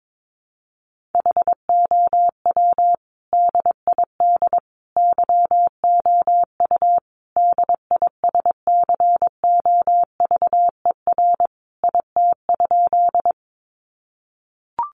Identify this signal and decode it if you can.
Morse code, heard as a keyed tone, about 22 wpm, 700 Hz. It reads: HOW DID YOU DISCOVER IT?
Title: Sentences from Top 1100 Words 22wpm